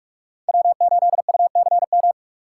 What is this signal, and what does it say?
Morse code, heard as a keyed tone, about 48 wpm, 700 Hz. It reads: W8UCM
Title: USA Call Signs - 5 Characters - 4x Character Spacing - 45wpm